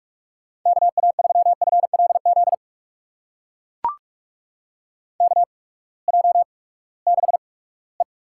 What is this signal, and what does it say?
Morse code, heard as a keyed tone, about 44 wpm, spaced as for 22 wpm, 700 Hz. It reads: KA3FLZ KJ6E
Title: USA Call Signs - 6 Characters - 8x Character Spacing - 45wpm